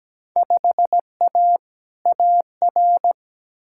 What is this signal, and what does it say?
Morse code, heard as a keyed tone, about 17 wpm, 700 Hz. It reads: 5A AR
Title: Field Day Exchanges Encore - 17wpm